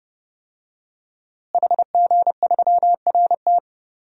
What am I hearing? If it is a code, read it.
Morse code, heard as a keyed tone, about 30 wpm, 700 Hz. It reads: HG3RT
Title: DXCC-100 Call Signs - 5 Characters - Repeated 3x - 30wpm